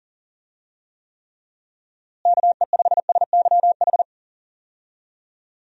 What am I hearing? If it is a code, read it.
Morse code, heard as a keyed tone, about 37 wpm, 700 Hz. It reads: KE5SYH
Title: USA Call Signs - 6 Characters - Repeated 2x - 40wpm